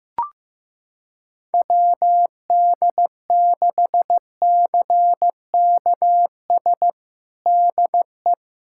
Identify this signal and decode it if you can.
Morse code, heard as a keyed tone, about 16 wpm, 700 Hz. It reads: WD6CKS DE
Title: USA Callsign DE Callsign - 6 Characters - 15wpm